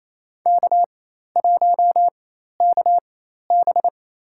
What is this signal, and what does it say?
Morse code, heard as a keyed tone, about 27 wpm, spaced as for 13 wpm, 700 Hz. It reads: K1KB
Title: USA Call Signs - 4 Characters - 4x Character Spacing - 28wpm